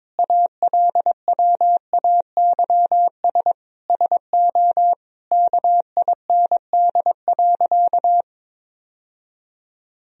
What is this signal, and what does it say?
Morse code, heard as a keyed tone, about 22 wpm, 700 Hz. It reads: ALWAYS SO KIND.
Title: Sentences from Top 1000 Words - Review Part 3 - 22wpm